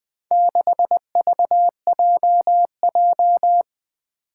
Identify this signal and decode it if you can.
Morse code, heard as a keyed tone, about 20 wpm, 700 Hz. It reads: 6VJJ
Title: Four Challenging Letter-Number Combinations 20wpm